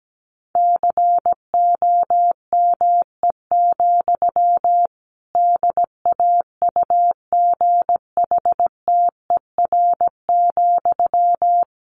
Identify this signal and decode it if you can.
Morse code, heard as a keyed tone, about 17 wpm, 700 Hz. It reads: COME, DAUGHTER,
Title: Sentences from Top 1500 Words 17wpm